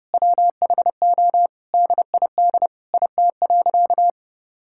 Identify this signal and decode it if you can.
Morse code, heard as a keyed tone, about 30 wpm, 700 Hz. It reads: WHO DID IT.
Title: Sentences from Top 1000 Words - Review Part 6 - 30wpm